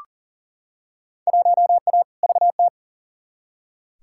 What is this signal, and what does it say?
Morse code, heard as a keyed tone, about 40 wpm, 700 Hz. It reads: 1A VT
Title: Field Day Exchanges Encore - 40wpm